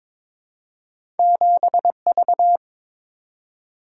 Morse code, heard as a keyed tone, about 22 wpm, 700 Hz. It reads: 7V
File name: practice / 22wpm / Two Letter-Number Combinations - 22wpm